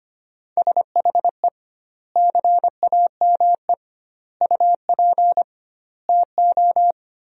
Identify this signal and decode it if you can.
Morse code, heard as a keyed tone, about 25 wpm, 700 Hz. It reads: SHE CAME UP TO
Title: 2x Word-Spacing - Sentences from Top 200 Words - 25wpm